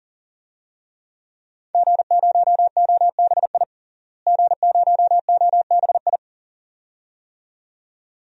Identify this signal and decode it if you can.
Morse code, heard as a keyed tone, about 40 wpm, 700 Hz. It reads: G0OBI G0OBI
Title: DXCC-100 Call Signs - 5 Characters - Repeated 2x - 40wpm